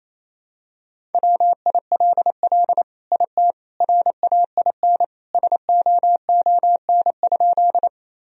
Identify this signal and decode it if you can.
Morse code, heard as a keyed tone, about 28 wpm, 700 Hz. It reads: WILL IT RAIN SOON?
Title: Sentences from Top 1000 Words - Review Part 1 - 28wpm